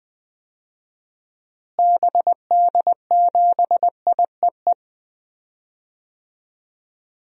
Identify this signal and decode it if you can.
Morse code, heard as a keyed tone, about 20 wpm, 700 Hz. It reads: BD7IEE
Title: DXCC-100 Call Signs - 6 Characters - Repeated 2x - 20wpm